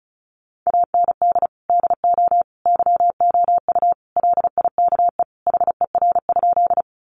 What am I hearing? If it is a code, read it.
Morse code, heard as a keyed tone, about 34 wpm, 700 Hz. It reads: AND DO YOU LIKE HER?
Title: Sentences from Top 100 Words 35wpm